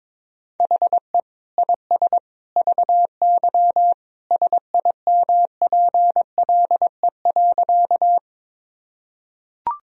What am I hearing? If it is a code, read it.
Morse code, heard as a keyed tone, about 21 wpm, 700 Hz. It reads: HE IS VY SIMPLE.
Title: Sentences from Top 400 Words and Abbreviations 22wpm